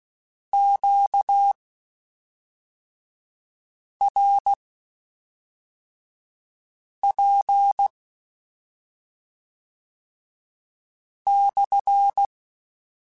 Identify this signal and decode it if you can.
Morse code, heard as a keyed tone, about 16 wpm, spaced as for 3 wpm, 800 Hz. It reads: QRP/
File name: Q Calls / Q Calls-mp3s / Q Calls SPSL1 @5